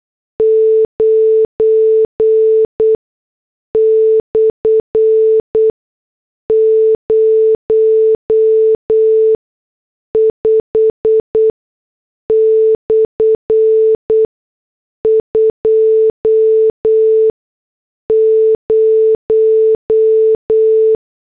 Morse code, heard as a keyed tone, about 8 wpm, spaced as for 6 wpm, 440 Hz. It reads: 9/05/20